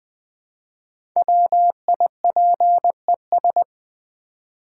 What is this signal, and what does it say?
Morse code, heard as a keyed tone, about 20 wpm, 700 Hz. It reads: WIPES
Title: Easy Words with Hints - 20wpm